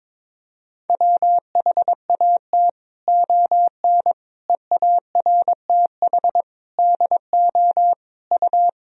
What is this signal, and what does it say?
Morse code, heard as a keyed tone, about 22 wpm, 700 Hz. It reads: WHAT ON EARTH DO U
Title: Sentences from Top 200 Words and Abbreviations 22wpm